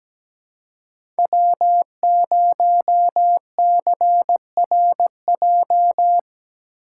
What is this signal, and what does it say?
Morse code, heard as a keyed tone, about 17 wpm, 700 Hz. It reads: W0CRJ
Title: USA Call Signs - 5 Characters - 4x Character Spacing - 17wpm